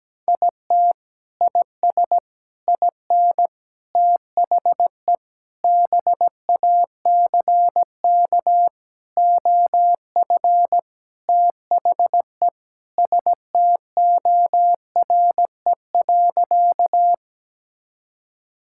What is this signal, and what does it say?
Morse code, heard as a keyed tone, about 17 wpm, 700 Hz. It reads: IT IS IN THE BACK OF THE STORE.
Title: Sentences from Top 1000 Words - Review Part 5 - 17wpm